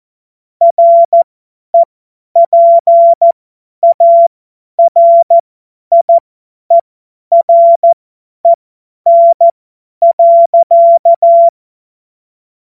Morse code, heard as a keyed tone, about 14 wpm, spaced as for 10 wpm, 680 Hz. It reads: REPARIEREN.
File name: Morsekurs / Witze / Witz 13 Februar 2024 50 F70